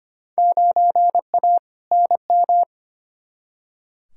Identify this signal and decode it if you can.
Morse code, heard as a keyed tone, about 25 wpm, 700 Hz. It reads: 9A NM
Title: Field Day Exchanges Encore - 25wpm